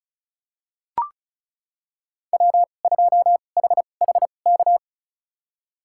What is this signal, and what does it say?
Morse code, heard as a keyed tone, about 36 wpm, spaced as for 25 wpm, 700 Hz. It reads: W2HHK
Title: USA Call Signs - 5 Characters - 2x Character Spacing - 35wpm